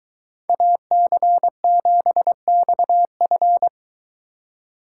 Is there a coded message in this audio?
Yes: Morse code from 0.5 to 3.7 s, heard as a keyed tone, about 23 wpm, 700 Hz. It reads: AC7XF